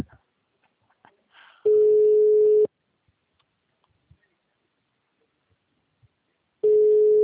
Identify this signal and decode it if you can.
Ringback tone (Europe, 425 Hz).